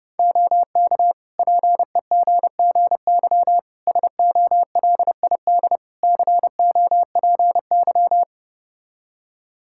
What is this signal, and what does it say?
Morse code, heard as a keyed tone, about 30 wpm, 700 Hz. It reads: OK PEGGY SOLID COPY
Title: Practice QSO - Exchange 3 - More Info - 30wpm